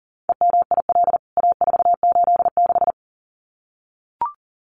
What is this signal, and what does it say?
Morse code, heard as a keyed tone, about 40 wpm, 700 Hz. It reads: EMIL A486
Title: CWT Exchange 40wpm